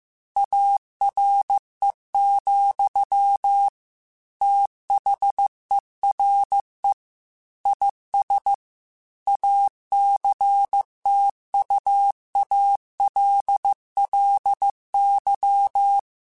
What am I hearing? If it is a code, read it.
Morse code, heard as a keyed tone, about 15 wpm, 800 Hz. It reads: ARE, THERE IS ACTUALLY
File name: Technical Subjects / Technical Subjects-mp3s / AND Gate @13